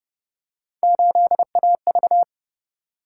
Morse code, heard as a keyed tone, about 29 wpm, 700 Hz. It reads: 8AV